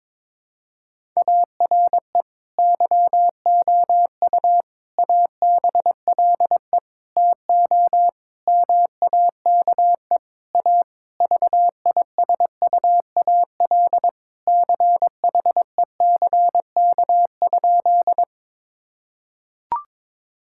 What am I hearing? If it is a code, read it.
Morse code, heard as a keyed tone, about 22 wpm, 700 Hz. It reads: ARE YOU ABLE TO MAKE A VISUAL CHECK?
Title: Sentences from Top 1000 Words - Review Part 6 - 22wpm